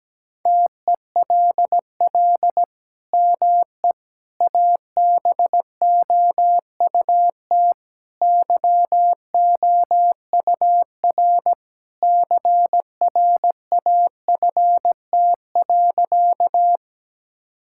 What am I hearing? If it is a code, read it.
Morse code, heard as a keyed tone, about 17 wpm, 700 Hz. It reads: TELL ME ABOUT YOUR CRAFT.
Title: Sentences from Top 1000 Words - Review Part 6 - 17wpm